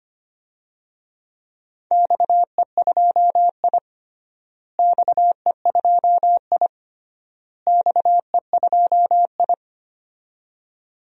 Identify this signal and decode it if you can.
Morse code, heard as a keyed tone, about 25 wpm, 700 Hz. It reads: XE2I XE2I XE2I
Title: DXCC-100 Call Signs - 4 Characters - Repeated 3x - 25wpm